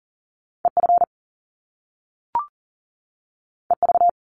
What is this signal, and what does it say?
Morse code, heard as a keyed tone, about 41 wpm, 700 Hz. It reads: EF EV
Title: Top Two Letter Combinations in Words 40wpm